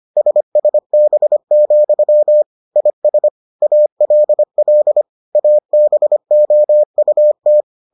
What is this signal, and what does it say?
Morse code, heard as a keyed tone, about 25 wpm, 600 Hz. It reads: SSB, IS ALL ABOUT